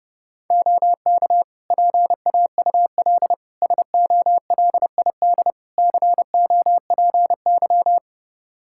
Morse code, heard as a keyed tone, about 28 wpm, 700 Hz. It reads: OK PAUL SOLID COPY
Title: K1USN SST QSO Practice 30wpm